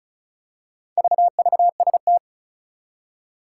Morse code, heard as a keyed tone, about 36 wpm, 700 Hz. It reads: VVST
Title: Four Challenging Letter-Number Combinations 35wpm